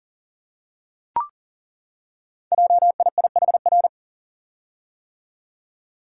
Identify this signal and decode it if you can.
Morse code, heard as a keyed tone, about 40 wpm, 690 Hz. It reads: JIIHR
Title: Code Groups - 5 Letters - 40wpm